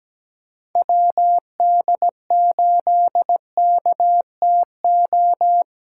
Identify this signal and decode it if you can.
Morse code, heard as a keyed tone, about 17 wpm, 700 Hz. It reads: WD8KTO
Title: USA Callsign DE Callsign - 6 Characters - 17wpm